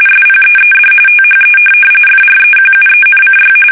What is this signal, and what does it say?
RTTY, about 45.45 baud. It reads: SPEED THE FREQUENCY
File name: fsk850